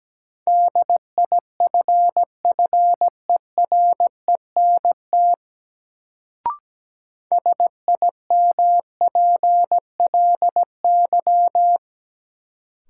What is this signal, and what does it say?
Morse code, heard as a keyed tone, about 17 wpm, 700 Hz. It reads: DIFFERENT SIMPLY
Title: Sentences from Top 600 Words - New Word Drill - 17wpm